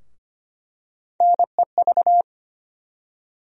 Morse code, heard as a keyed tone, about 25 wpm, 700 Hz. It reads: NEV
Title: USA Call Sign Postfixes - 3 Letters 25wpm